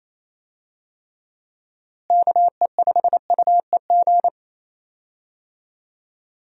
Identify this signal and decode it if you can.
Morse code, heard as a keyed tone, about 28 wpm, 700 Hz. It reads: KE5UEG
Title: USA Call Signs - 6 Characters - Repeated 2x - 28wpm